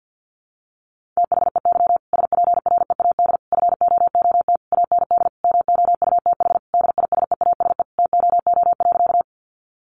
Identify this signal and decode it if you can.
Morse code, heard as a keyed tone, about 50 wpm, 700 Hz. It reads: THEY SPREAD FOOT AND MOUTH DISEASE TOO.
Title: Sentences from Top 1500 Words 50wpm